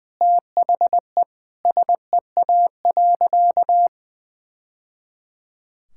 Morse code, heard as a keyed tone, about 20 wpm, 700 Hz. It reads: THE SEA.
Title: Sentences from Top 1000 Words - Review Part 6 - 20wpm